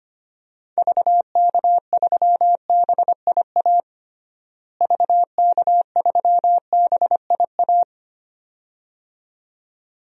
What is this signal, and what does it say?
Morse code, heard as a keyed tone, about 25 wpm, 700 Hz. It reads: VK3BIA VK3BIA
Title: DXCC-100 Call Signs - 6 Characters - Repeated 2x - 25wpm